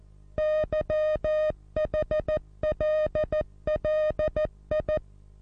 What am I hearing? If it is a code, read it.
Morse code, heard as a keyed tone, about 14 wpm, 620 Hz. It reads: YHLLI
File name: repos / Morsekurs / DJ4UF Morsen27